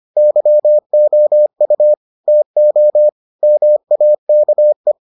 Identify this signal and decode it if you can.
Morse code, heard as a keyed tone, about 25 wpm, 600 Hz. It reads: YOU TO MAKE